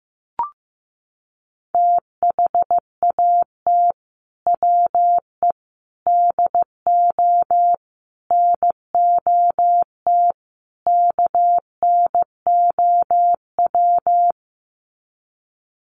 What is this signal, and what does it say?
Morse code, heard as a keyed tone, about 15 wpm, 700 Hz. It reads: THAT WE DO NOT KNOW
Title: Sets of 5 Words Encore2 15wpm